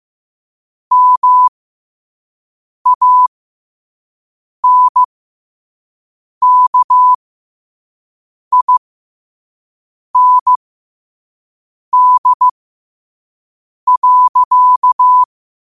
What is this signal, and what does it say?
Morse code, heard as a keyed tone, about 15 wpm, spaced as for 5 wpm, 1000 Hz. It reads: MANKIND.
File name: 5wpm